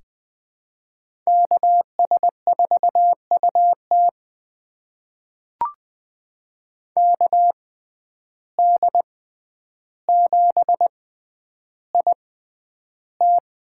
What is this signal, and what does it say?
Morse code, heard as a keyed tone, about 20 wpm, spaced as for 10 wpm, 700 Hz. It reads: KS4UT KD7IT
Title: USA Call Signs - 5 Characters - 6x Character Spacing - 20wpm